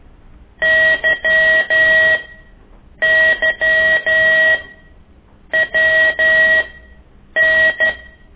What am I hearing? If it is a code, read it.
Morse code, heard as a keyed tone, about 12 wpm, spaced as for 7 wpm, 1900 Hz, with noise. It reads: YYWN
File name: ik1weg2 / CORSI CW